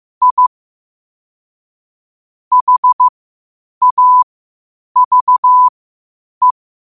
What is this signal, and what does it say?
Morse code, heard as a keyed tone, about 15 wpm, spaced as for 8 wpm, 1000 Hz. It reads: I HAVE
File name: qotdfiles / 8wpm